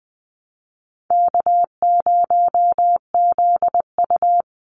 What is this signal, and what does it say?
Morse code, heard as a keyed tone, about 20 wpm, 700 Hz. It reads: K0ZU
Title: Easy Call Signs Repeated 3x - 20wpm